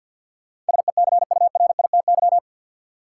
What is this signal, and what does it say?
Morse code, heard as a keyed tone, about 53 wpm, 700 Hz. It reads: SECURITY
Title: Sentences from Top 1000 Words - Review Part 4 - New Word Drill - 50wpm